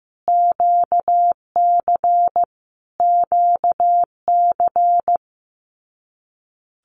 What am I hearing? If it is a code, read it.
Morse code, heard as a keyed tone, about 15 wpm, 700 Hz. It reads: QC QC
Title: Canadian Provinces and Territories - Repeated 3x - 15wpm